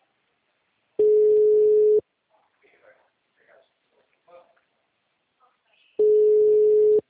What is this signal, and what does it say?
Ringback tone (Europe, 425 Hz), with speech.